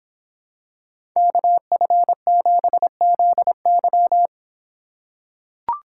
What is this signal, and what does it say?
Morse code, heard as a keyed tone, about 25 wpm, 700 Hz. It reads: KF7ZY